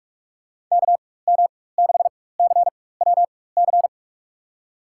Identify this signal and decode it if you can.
Morse code, heard as a keyed tone, about 46 wpm, spaced as for 21 wpm, 700 Hz. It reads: KM6CWC
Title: USA Call Signs - 6 Characters - 4x Character Spacing - 45wpm